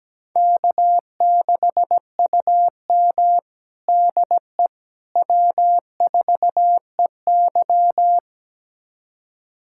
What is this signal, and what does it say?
Morse code, heard as a keyed tone, about 17 wpm, 700 Hz. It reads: K6UM DE W4EY
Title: USA Callsign DE Callsign - 4 Characters - 17wpm